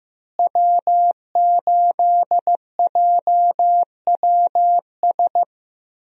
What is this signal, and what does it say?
Morse code, heard as a keyed tone, about 15 wpm, 700 Hz. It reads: W8JWS
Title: USA Call Signs - 5 Characters - Repeated 2x - 15wpm